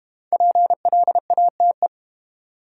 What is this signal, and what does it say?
Morse code, heard as a keyed tone, about 32 wpm, 700 Hz. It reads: PLATE